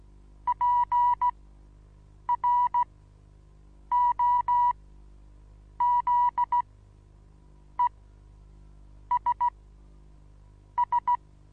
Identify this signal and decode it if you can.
Morse code, heard as a keyed tone, about 16 wpm, spaced as for 6 wpm, 970 Hz. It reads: PROZESS
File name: Onlinelehrgang / morsen / DJ4UF Morsen25